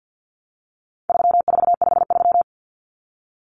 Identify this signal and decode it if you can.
Morse code, heard as a keyed tone, about 50 wpm, 700 Hz. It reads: 3453
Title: Numbers - 4-Digit 50wpm